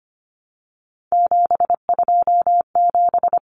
Morse code, heard as a keyed tone, about 25 wpm, 700 Hz. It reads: 727
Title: Numbers - 3-Digit 25wpm